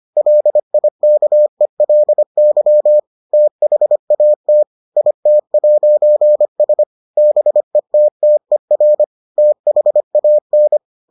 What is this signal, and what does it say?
Morse code, heard as a keyed tone, about 25 wpm, 600 Hz. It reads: LIKELY THAT IT'S BETTER THAN